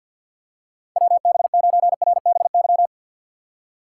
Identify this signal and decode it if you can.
Morse code, heard as a keyed tone, about 53 wpm, 690 Hz. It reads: WB9RBY
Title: USA Call Signs - 2x Character Spacing - 50wpm